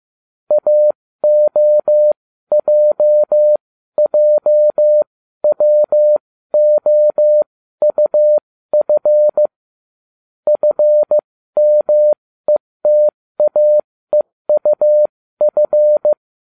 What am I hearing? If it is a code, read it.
Morse code, heard as a keyed tone, about 15 wpm, spaced as for 13 wpm, 600 Hz. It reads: AOJJWOUF FMETAEUF